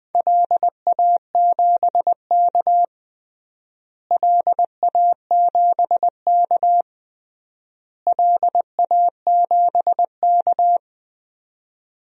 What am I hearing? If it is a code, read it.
Morse code, heard as a keyed tone, about 21 wpm, 700 Hz. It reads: LA7K LA7K LA7K
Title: DXCC-100 Call Signs - 4 Characters - Repeated 3x - 20wpm